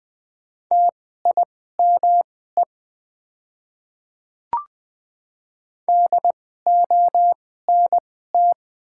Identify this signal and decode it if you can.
Morse code, heard as a keyed tone, about 20 wpm, spaced as for 15 wpm, 700 Hz. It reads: TIME DONT
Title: 2x Character Spacing - Top 300 Words - 20wpm